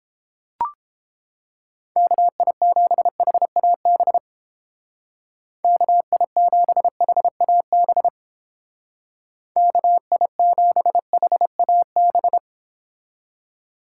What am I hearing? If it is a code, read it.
Morse code, heard as a keyed tone, about 31 wpm, 700 Hz. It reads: KI7HAB KI7HAB KI7HAB